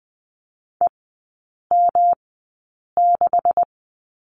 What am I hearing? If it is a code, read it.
Morse code, heard as a keyed tone, about 20 wpm, spaced as for 8 wpm, 700 Hz. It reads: EM6
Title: 127 Lesson 21e - Number 6 - Three Characters 20wpm